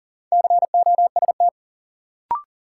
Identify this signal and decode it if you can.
Morse code, heard as a keyed tone, about 40 wpm, 700 Hz. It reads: COST
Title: Sets of 3 Words - New Word Drill - 40wpm